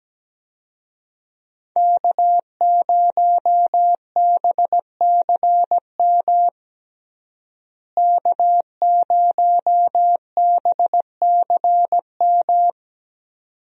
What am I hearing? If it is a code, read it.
Morse code, heard as a keyed tone, about 17 wpm, 700 Hz. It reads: K0BCM K0BCM
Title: USA Call Signs - 5 Characters - Repeated 2x - 17wpm